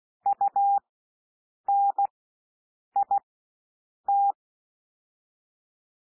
Morse code, heard as a keyed tone, about 16 wpm, spaced as for 7 wpm, 800 Hz. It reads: UNIT
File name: SOUND WORDS 16 wpm